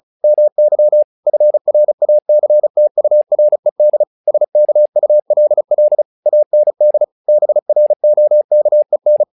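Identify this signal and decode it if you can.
Morse code, heard as a keyed tone, about 34 wpm, 600 Hz. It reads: MY FRACTURED SKULL AND BROKEN
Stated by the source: Morse practice text "Tower Trouble" sent at 35 WPM, from the SKCC - Straight Key Century Club learning center audio files.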